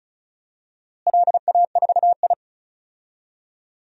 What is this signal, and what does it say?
Morse code, heard as a keyed tone, about 34 wpm, 700 Hz. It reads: LA4I
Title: DXCC-100 Call Signs - 4 Characters - Repeated 3x - 35wpm